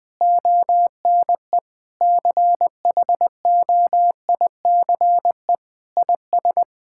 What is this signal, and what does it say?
Morse code, heard as a keyed tone, about 20 wpm, 700 Hz. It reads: ONE CHOICE IS
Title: Sentences from Top 1000 Words - Review Part 6 - 20wpm